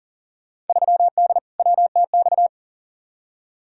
Morse code, heard as a keyed tone, about 40 wpm, 700 Hz. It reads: 3D WTX
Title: Field Day Exchanges Encore - 40wpm